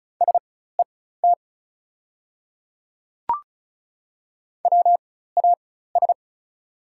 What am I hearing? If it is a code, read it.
Morse code, heard as a keyed tone, about 36 wpm, spaced as for 16 wpm, 690 Hz. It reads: SET WAS
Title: 4x Character Spacing - Top 200 Words - 35wpm